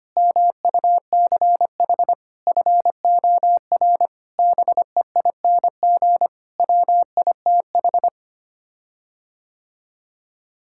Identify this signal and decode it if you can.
Morse code, heard as a keyed tone, about 25 wpm, 700 Hz. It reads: MUCH FOR BEING WITH
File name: Sets of 9 Words - 25wpm